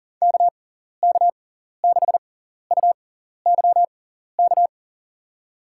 Morse code, heard as a keyed tone, about 37 wpm, spaced as for 14 wpm, 700 Hz. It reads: KK6UYK
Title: USA Call Signs - 6 Characters - 6x Character Spacing - 40wpm